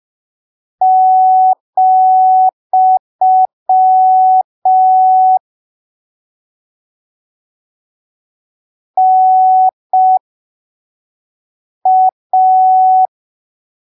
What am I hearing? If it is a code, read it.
Morse code, heard as a keyed tone, about 5 wpm, spaced as for 3 wpm, 750 Hz. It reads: , NA
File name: arquivos / texto / texto 19 05ppm